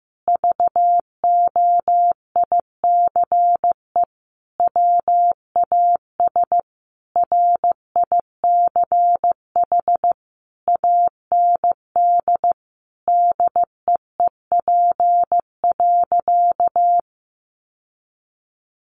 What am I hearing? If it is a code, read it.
Morse code, heard as a keyed tone, about 15 wpm, 700 Hz. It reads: VOICE WAS RICH AND DEEP.
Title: Sentences from Top 1100 Words 15wpm